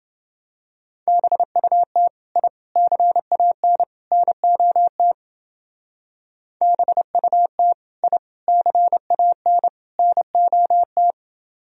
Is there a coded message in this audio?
Yes: Morse code from 1.1 to 11.1 s, heard as a keyed tone, about 29 wpm, 700 Hz. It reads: BUT I CAN NOT BUT I CAN NOT